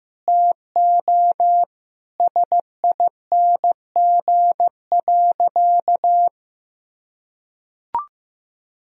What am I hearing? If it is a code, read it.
Morse code, heard as a keyed tone, about 16 wpm, 700 Hz. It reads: TO SING.
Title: Sentences from Top 1000 Words - Review Part 2 - 15wpm